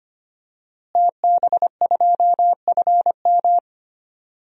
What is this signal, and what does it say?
Morse code, heard as a keyed tone, about 25 wpm, 700 Hz. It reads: TB2FM
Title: DXCC-100 Call Signs - 5 Characters - Repeated 2x - 25wpm